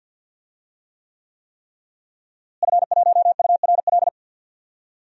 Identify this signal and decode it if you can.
Morse code, heard as a keyed tone, about 52 wpm, 700 Hz. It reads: F1URL
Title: DXCC-100 Call Signs - 5 Characters - Repeated 3x - 50wpm